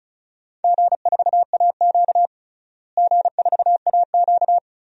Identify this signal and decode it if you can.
Morse code, heard as a keyed tone, about 36 wpm, 700 Hz. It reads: G4AQ G4AQ
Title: DXCC-100 Call Signs - 4 Characters - Repeated 2x - 35wpm